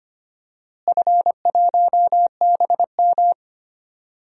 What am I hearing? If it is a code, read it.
Morse code, heard as a keyed tone, about 25 wpm, 700 Hz. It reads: F1BM
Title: DXCC-100 Call Signs - 4 Characters - Repeated 2x - 25wpm